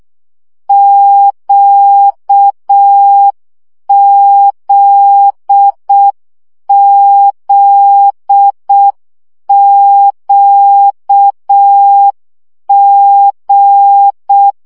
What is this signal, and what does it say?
Morse code, heard as a keyed tone, about 6 wpm, 800 Hz. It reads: QZZQG
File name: CW gyakorlas g z q 5 baud